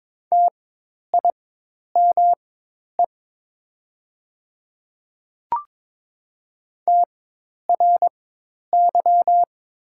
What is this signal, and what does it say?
Morse code, heard as a keyed tone, about 22 wpm, spaced as for 10 wpm, 700 Hz. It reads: TIME TRY
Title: 4x Character Spacing - Top 200 Words - 22wpm